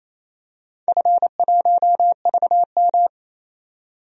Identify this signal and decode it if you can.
Morse code, heard as a keyed tone, about 28 wpm, 700 Hz. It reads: F1VM
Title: DXCC-100 Call Signs - 4 Characters - 28wpm